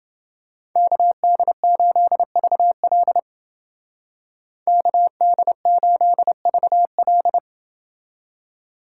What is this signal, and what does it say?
Morse code, heard as a keyed tone, about 30 wpm, 700 Hz. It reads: KD8VL KD8VL